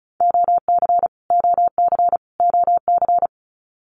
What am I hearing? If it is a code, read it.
Morse code, heard as a keyed tone, about 34 wpm, 700 Hz. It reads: OC OC OC
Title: Abbreviations Repeated 3x - 35wpm